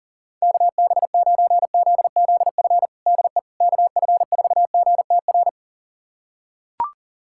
Morse code, heard as a keyed tone, about 40 wpm, 700 Hz. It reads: KB9ZZF DE KF4GTR
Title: USA Callsign DE Callsign - 6 Characters - 40wpm